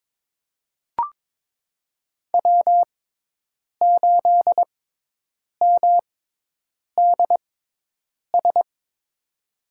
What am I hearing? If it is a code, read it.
Morse code, heard as a keyed tone, about 22 wpm, spaced as for 8 wpm, 700 Hz. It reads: W8MDS
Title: USA Call Signs - 6x Character Spacing - 22wpm